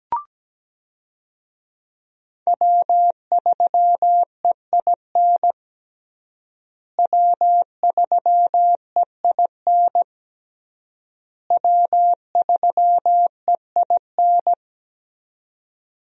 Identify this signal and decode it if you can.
Morse code, heard as a keyed tone, about 17 wpm, 700 Hz. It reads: W3EIN W3EIN W3EIN
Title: USA Call Signs - 5 Characters - Repeated 3x - 17wpm